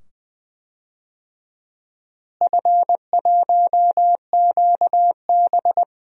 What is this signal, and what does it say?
Morse code, heard as a keyed tone, about 20 wpm, 700 Hz. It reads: F1QB
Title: DXCC-100 Call Signs - 4 Characters - Repeated 2x - 20wpm